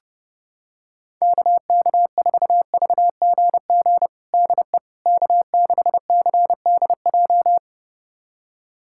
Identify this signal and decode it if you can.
Morse code, heard as a keyed tone, about 30 wpm, 700 Hz. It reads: KK4VGG DE K6CDJ
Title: USA Callsign DE Callsign 30wpm